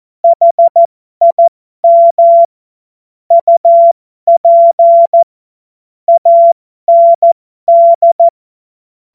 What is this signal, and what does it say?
Morse code, heard as a keyed tone, about 14 wpm, spaced as for 12 wpm, 680 Hz. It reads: HIM UP AND